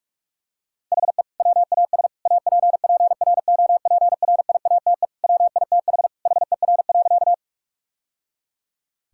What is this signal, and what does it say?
Morse code, heard as a keyed tone, about 46 wpm, 700 Hz. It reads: HE WAS APPROPRIATE WITH HER.
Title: Sentences from Top 1000 Words - Review Part 5 - 45wpm